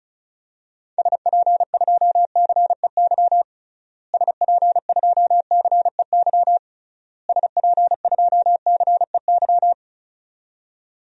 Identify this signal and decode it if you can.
Morse code, heard as a keyed tone, about 34 wpm, 700 Hz. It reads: SP2CEY SP2CEY SP2CEY
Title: DXCC-100 Call Signs - 6 Characters - Repeated 3x - 35wpm